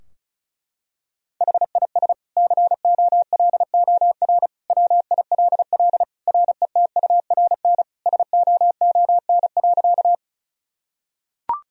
Morse code, heard as a keyed tone, about 34 wpm, 700 Hz. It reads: HIS COLOR WILL RETURN SOON.
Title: Sentences from Top 1000 Words - Review Part 2 - 35wpm